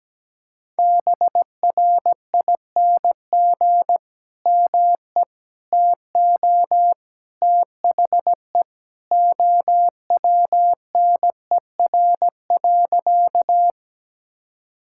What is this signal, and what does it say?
Morse code, heard as a keyed tone, about 17 wpm, 700 Hz. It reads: BRING ME TO THE OWNER.
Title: Sentences from Top 1600 Words 17wpm